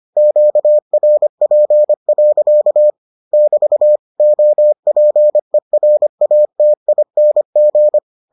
Morse code, heard as a keyed tone, about 25 wpm, 600 Hz. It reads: QRP. <BT> OPERATING